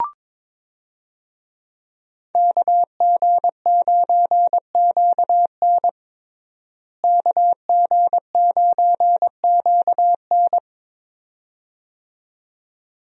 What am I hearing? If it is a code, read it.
Morse code, heard as a keyed tone, about 21 wpm, 700 Hz. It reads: KG9QN KG9QN
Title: USA Call Signs - 5 Characters - Repeated 2x - 22wpm